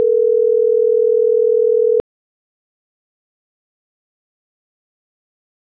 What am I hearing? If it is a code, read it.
Ringback tone (North America), 440 + 480 Hz.